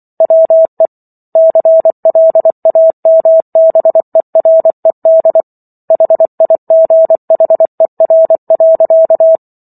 Morse code, heard as a keyed tone, about 24 wpm, 650 Hz. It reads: WE CLAMBERED HIGHER.
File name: Book One / 15-Chapter 15 - 24 WPM